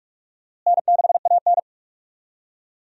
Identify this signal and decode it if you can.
Morse code, heard as a keyed tone, about 43 wpm, 700 Hz. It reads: N6AN
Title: USA Call Signs - 4 Characters - Repeated 2x - 45wpm